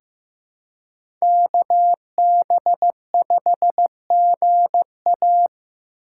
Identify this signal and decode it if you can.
Morse code, heard as a keyed tone, about 15 wpm, 700 Hz. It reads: KB5GA
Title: K1USN SST QSO Practice - 15wpm